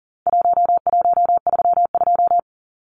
Morse code, heard as a keyed tone, about 40 wpm, 700 Hz. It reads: 1132
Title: Numbers - Contest Style - 4-Digit 40wpm